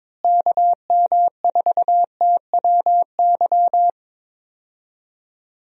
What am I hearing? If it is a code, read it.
Morse code, heard as a keyed tone, about 22 wpm, 700 Hz. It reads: KM4TWY
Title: USA Call Signs - 6 Characters - Repeated 2x - 22wpm